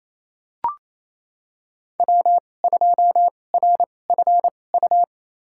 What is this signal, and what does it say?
Morse code, heard as a keyed tone, about 28 wpm, spaced as for 20 wpm, 700 Hz. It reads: W2RFU
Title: USA Call Signs - 5 Characters - 2x Character Spacing - 28wpm